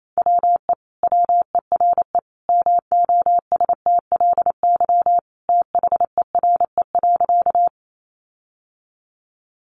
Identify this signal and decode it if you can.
Morse code, heard as a keyed tone, about 28 wpm, 700 Hz. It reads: WE WERE MOSTLY THERE.
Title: Sentences from Top 1300 Words 28wpm